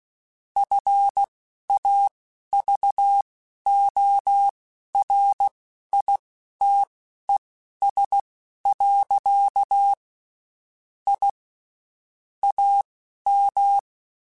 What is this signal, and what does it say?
Morse code, heard as a keyed tone, about 16 wpm, spaced as for 11 wpm, 800 Hz. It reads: FAVORITES. I AM
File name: (spaced) / QSOs Spaced-mp3s / WD0FBT 1 @15 sp